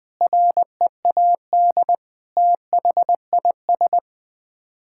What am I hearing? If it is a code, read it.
Morse code, heard as a keyed tone, about 20 wpm, 700 Hz. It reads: READ THIS
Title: Sets of 9 Words - 20wpm